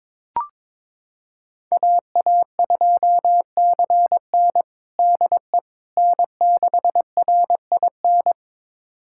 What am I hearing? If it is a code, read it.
Morse code, heard as a keyed tone, about 22 wpm, 700 Hz. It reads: AA2CN DE N6RIN
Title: USA Callsign DE Callsign - 5 Characters - 22wpm